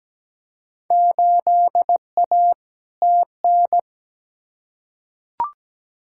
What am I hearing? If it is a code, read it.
Morse code, heard as a keyed tone, about 17 wpm, 700 Hz. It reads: 8A TN
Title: Field Day Exchanges Encore - 17wpm